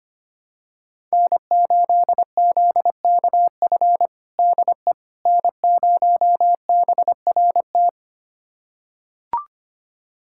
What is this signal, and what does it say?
Morse code, heard as a keyed tone, about 25 wpm, 700 Hz. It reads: N8ZKF DE N0BRT
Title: USA Callsign DE Callsign - 5 Characters - 25wpm